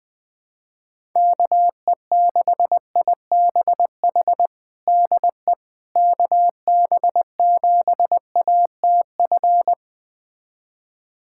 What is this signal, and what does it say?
Morse code, heard as a keyed tone, about 20 wpm, 700 Hz. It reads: KE6IBH DE KB7ATF
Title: USA Callsign DE Callsign - 6 Characters - 20wpm